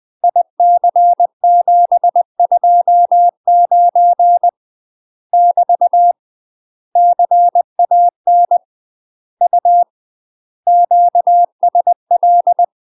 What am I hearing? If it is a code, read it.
Morse code, heard as a keyed tone, about 20 wpm, 700 Hz. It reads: IC729 <BT> CAN U QSL
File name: UK MASTER QSOs TRAINING / 20WPM / UK MASTER QSOs 0017